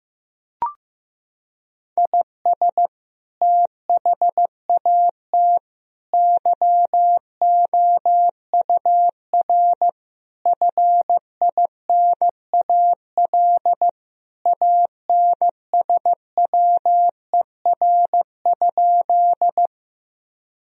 Morse code, heard as a keyed tone, about 15 wpm, 700 Hz. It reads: IS THAT YOUR FINAL ANSWER?
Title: Sentences from Top 1000 Words - Review Part 2 - 15wpm